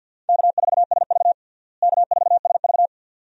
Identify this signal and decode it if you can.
Morse code, heard as a keyed tone, about 46 wpm, 700 Hz. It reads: K4SV K4SV
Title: USA Call Signs - 4 Characters - Repeated 2x - 50wpm